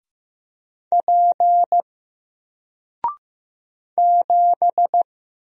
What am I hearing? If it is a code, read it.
Morse code, heard as a keyed tone, about 15 wpm, 700 Hz. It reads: P 7
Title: Single Character (Letter:Number:Punctuation:Prosign) 15wpm